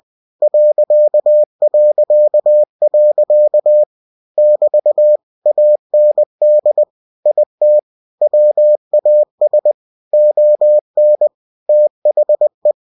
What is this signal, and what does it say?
Morse code, heard as a keyed tone, about 20 wpm, 600 Hz. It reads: ... <BT> AND IT WAS ON THE
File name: cw-books / winnie-the-pooh / winnie-20wpm-page-0094